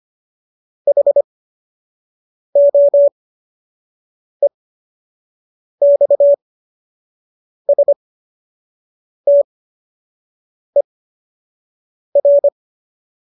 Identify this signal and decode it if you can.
Morse code, heard as a keyed tone, about 25 wpm, spaced as for 6 wpm, 570 Hz. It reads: HOEXSTER
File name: qso012 06